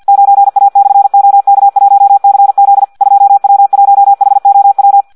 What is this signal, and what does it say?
Morse code, heard as a keyed tone, about 50 wpm, 800 Hz. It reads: 9A/OK1CZ 2W1HOW
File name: Audio practica CW / qrq-tx-50 2